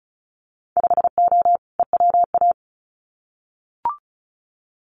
Morse code, heard as a keyed tone, about 34 wpm, 700 Hz. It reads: 5O EWA
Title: Winter Field Day Exchanges Encore 35wpm